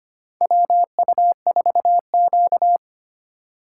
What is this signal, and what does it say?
Morse code, heard as a keyed tone, about 25 wpm, 700 Hz. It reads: WU4Q
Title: USA Call Signs - 4 Characters - 4x Character Spacing - 25wpm